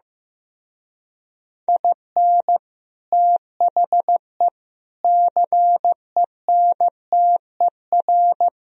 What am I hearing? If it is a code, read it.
Morse code, heard as a keyed tone, about 16 wpm, 700 Hz. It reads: IN THE CENTER
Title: Speed-Racing - Sets of 4 Words 15wpm